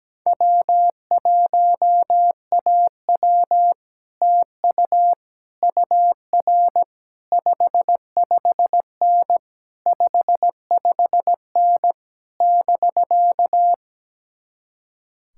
Morse code, heard as a keyed tone, about 17 wpm, 700 Hz. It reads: W1AW TU UR 55N 55N <BK>
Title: Speed-Racing Common QSO Phrases - 17wpm